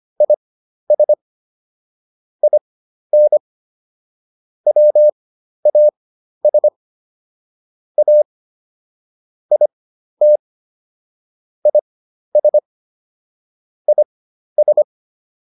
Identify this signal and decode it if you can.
Morse code, heard as a keyed tone, about 25 wpm, spaced as for 12 wpm, 600 Hz. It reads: IS IN WAS A IT IS IS